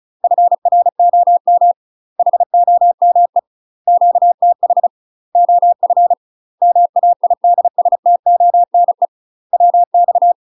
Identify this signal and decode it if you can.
Morse code, heard as a keyed tone, about 34 wpm, 700 Hz. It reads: FROM HOME QTH OF MAIDSTONE WX
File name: UK MASTER QSOs TRAINING / 35WPM / UK MASTER QSOs 0037